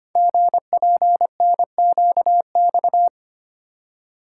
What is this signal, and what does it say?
Morse code, heard as a keyed tone, about 25 wpm, 700 Hz. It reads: GPNQX
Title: Code Groups - 5 Letter:Numbers - 25wpm